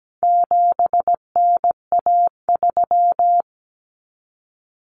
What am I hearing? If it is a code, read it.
Morse code, heard as a keyed tone, about 17 wpm, 700 Hz. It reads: 7NA3
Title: Common Cut Numbers - 4-Digit 17wpm